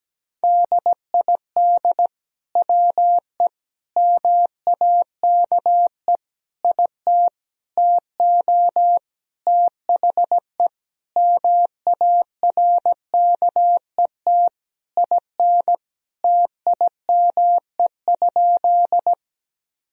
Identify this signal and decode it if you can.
Morse code, heard as a keyed tone, about 17 wpm, 700 Hz. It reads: DID WE MAKE IT TO THE MARKET IN TIME?
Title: Sentences from Top 1000 Words - Review Part 3 - 17wpm